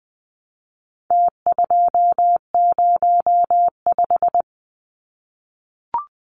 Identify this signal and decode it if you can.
Morse code, heard as a keyed tone, about 20 wpm, 700 Hz. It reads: T205
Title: Numbers - Contest Style - 4-Digit 20wpm